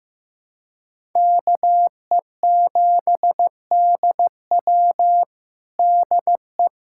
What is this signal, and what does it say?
Morse code, heard as a keyed tone, about 16 wpm, 700 Hz. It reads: KE7DW DE
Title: USA Callsign DE Callsign - 5 Characters - 15wpm